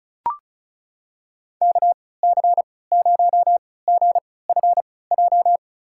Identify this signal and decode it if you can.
Morse code, heard as a keyed tone, about 34 wpm, spaced as for 20 wpm, 700 Hz. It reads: KC0GFJ
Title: USA Call Signs - 6 Characters - 3x Character Spacing - 35wpm